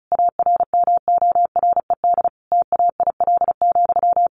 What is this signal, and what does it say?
Morse code, heard as a keyed tone, about 34 wpm, 700 Hz. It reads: ARMORED TAIL,
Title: A Sound Of Thunder 35wpm